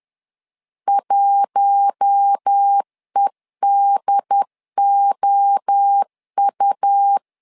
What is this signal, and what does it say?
Morse code, heard as a keyed tone, about 10 wpm, 800 Hz. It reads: 1EDOU
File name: LEZIONI / 4 Q1 10wpm 2448